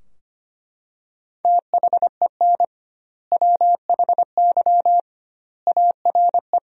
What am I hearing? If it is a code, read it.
Morse code, heard as a keyed tone, about 25 wpm, 700 Hz. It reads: THEN WHY ARE
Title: 2x Word-Spacing - Sentences from Top 200 Words - 25wpm